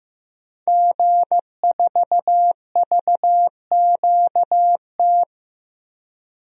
Morse code, heard as a keyed tone, about 15 wpm, 700 Hz. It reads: G4VQT
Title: DXCC-100 Call Signs - 5 Characters - 15wpm